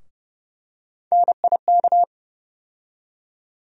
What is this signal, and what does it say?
Morse code, heard as a keyed tone, about 30 wpm, 700 Hz. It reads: NIK
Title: USA Call Sign Postfixes - 3 Letters 30wpm